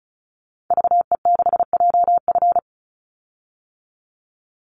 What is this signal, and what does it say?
Morse code, heard as a keyed tone, about 36 wpm, 700 Hz. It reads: VE6JF
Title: Call Signs - 35wpm